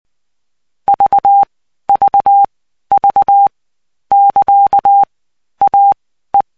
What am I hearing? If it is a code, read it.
Morse code, heard as a keyed tone, about 20 wpm, spaced as for 13 wpm, 800 Hz. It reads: VVV <KA> AE
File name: cw 75BPM